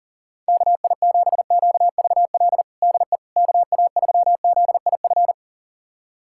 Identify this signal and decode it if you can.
Morse code, heard as a keyed tone, about 40 wpm, 700 Hz. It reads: KI7QVL DE KA3ZIF
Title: USA Callsign DE Callsign 40wpm